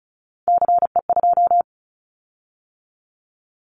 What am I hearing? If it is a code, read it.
Morse code, heard as a keyed tone, about 36 wpm, 700 Hz. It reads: CE2
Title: Callsign Prefixes for Top 100 DXCC - 35wpm